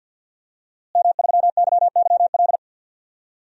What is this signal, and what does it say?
Morse code, heard as a keyed tone, about 46 wpm, 700 Hz. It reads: M3YYL
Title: DXCC-100 Call Signs - 5 Characters - Repeated 2x - 50wpm